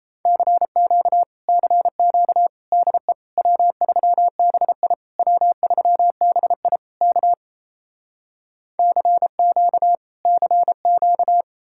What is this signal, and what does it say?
Morse code, heard as a keyed tone, about 37 wpm, 700 Hz. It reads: CQ CQ DE W3BI W3BI K CQ CQ